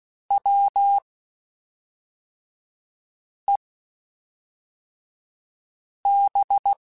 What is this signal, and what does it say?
Morse code, heard as a keyed tone, about 16 wpm, spaced as for 3 wpm, 800 Hz. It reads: WEB
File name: Quotes Spaced / Quotes Spaced-mp3s / Deceive sp @5